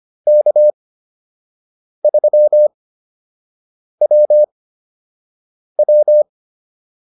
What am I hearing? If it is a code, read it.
Morse code, heard as a keyed tone, about 25 wpm, spaced as for 6 wpm, 600 Hz. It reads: K3WW